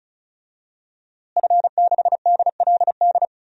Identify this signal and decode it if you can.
Morse code, heard as a keyed tone, about 36 wpm, 700 Hz. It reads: F6DLD
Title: DXCC-100 Call Signs - 5 Characters - Repeated 3x - 35wpm